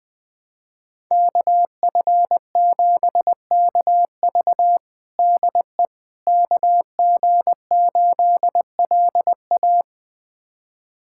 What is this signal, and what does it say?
Morse code, heard as a keyed tone, about 20 wpm, 700 Hz. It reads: KF7KV DE KG8LA
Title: USA Callsign DE Callsign - 5 Characters - 20wpm